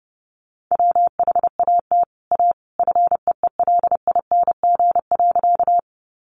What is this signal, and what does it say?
Morse code, heard as a keyed tone, about 30 wpm, 700 Hz. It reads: WHAT A FEELING.
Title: Sentences from Top 1200 Words 30wpm